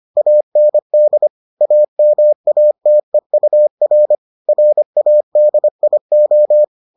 Morse code, heard as a keyed tone, about 25 wpm, 600 Hz. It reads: AND AMATEUR RADIO